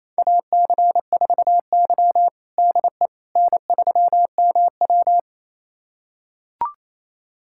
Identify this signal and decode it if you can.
Morse code, heard as a keyed tone, about 28 wpm, 700 Hz. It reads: AC4Y DE N3MW
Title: USA Callsign DE Callsign - 4 Characters - 28wpm